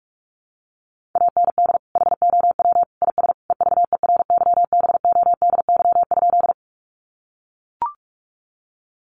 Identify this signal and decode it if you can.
Morse code, heard as a keyed tone, about 44 wpm, 700 Hz. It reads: AND HOW IS EVERYBODY?
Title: Sentences from Top 1400 Words 45wpm